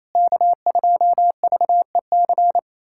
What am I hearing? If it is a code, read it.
Morse code, heard as a keyed tone, about 27 wpm, 700 Hz. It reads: K2VEC
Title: USA Callsign DE Callsign - 5 Characters - 28wpm